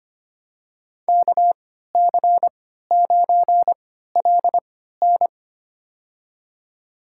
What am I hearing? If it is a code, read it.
Morse code, heard as a keyed tone, about 25 wpm, spaced as for 14 wpm, 700 Hz. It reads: KC9LN
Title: USA Call Signs - 5 Characters - 3x Character Spacing - 25wpm